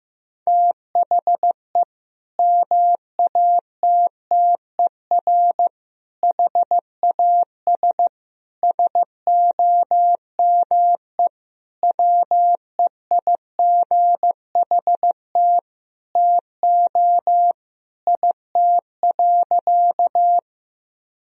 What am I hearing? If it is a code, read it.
Morse code, heard as a keyed tone, about 15 wpm, 700 Hz. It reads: THE MATTER HAS SOME WEIGHT TO IT.
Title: Sentences from Top 1000 Words - Review Part 3 - 15wpm